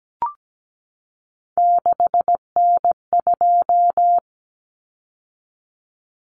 Morse code, heard as a keyed tone, about 17 wpm, 700 Hz. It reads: 6N2
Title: Common Cut Numbers - 3-Digit 17wpm